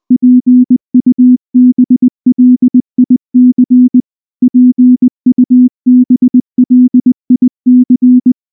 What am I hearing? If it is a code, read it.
Morse code, heard as a keyed tone, about 20 wpm, 250 Hz. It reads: PUBLIC PUBLIC
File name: morse / morse-training-20WPM-5